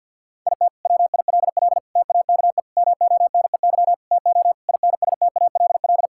Morse code, heard as a keyed tone, about 50 wpm, 700 Hz. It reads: IT WILL TAKE MONEY TO INSTALL
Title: Sentences from Top 1600 Words 50wpm